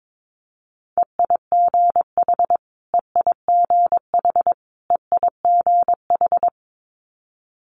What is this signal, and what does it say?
Morse code, heard as a keyed tone, about 22 wpm, 700 Hz. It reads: EIGH EIGH EIGH
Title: Multi-letter Phonemes Repeated 3x 22wpm